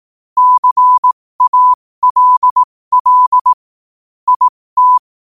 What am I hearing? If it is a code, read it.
Morse code, heard as a keyed tone, about 18 wpm, spaced as for 15 wpm, 1000 Hz. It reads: CALL IT